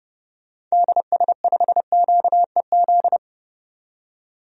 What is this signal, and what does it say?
Morse code, heard as a keyed tone, about 30 wpm, 700 Hz. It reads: DS5QEZ
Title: DXCC-100 Call Signs - 6 Characters - 30wpm